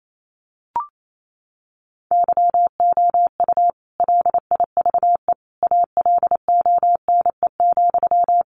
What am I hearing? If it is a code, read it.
Morse code, heard as a keyed tone, about 28 wpm, 700 Hz. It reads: YOU LIVE ALONE,
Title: Sentences from Top 1000 Words 28wpm